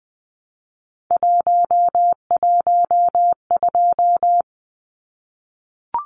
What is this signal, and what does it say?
Morse code, heard as a keyed tone, about 20 wpm, 700 Hz. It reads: 112
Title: Numbers - 3-Digit 20wpm